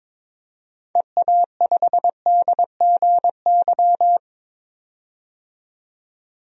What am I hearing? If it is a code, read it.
Morse code, heard as a keyed tone, about 22 wpm, 700 Hz. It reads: EA5DGY
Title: DXCC-100 Call Signs - 6 Characters - 22wpm